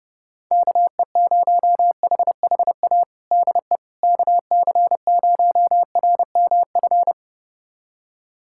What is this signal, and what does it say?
Morse code, heard as a keyed tone, about 30 wpm, 700 Hz. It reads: KE0HHA DE KC0RMF
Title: USA Callsign DE Callsign 30wpm